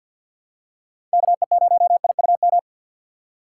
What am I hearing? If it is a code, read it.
Morse code, heard as a keyed tone, about 46 wpm, 700 Hz. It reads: KE0IUM
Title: USA Call Signs - 6 Characters - 4x Character Spacing - 50wpm